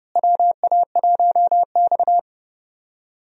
Morse code, heard as a keyed tone, about 30 wpm, 700 Hz. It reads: WA1X
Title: USA Call Signs - 4 Characters - Repeated 2x - 30wpm